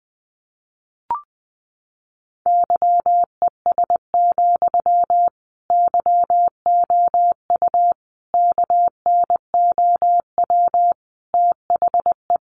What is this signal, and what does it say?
Morse code, heard as a keyed tone, about 20 wpm, 700 Hz. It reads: YES, YOU KNOW THE
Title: Sentences from Top 700 Words 20wpm